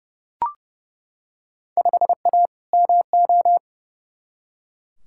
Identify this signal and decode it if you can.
Morse code, heard as a keyed tone, about 29 wpm, 700 Hz. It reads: 5A MO
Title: Field Day Exchanges Encore - 30wpm